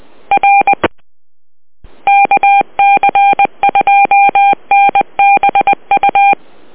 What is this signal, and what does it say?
Morse code, heard as a keyed tone, about 20 wpm, 800 Hz. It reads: R KC2NBU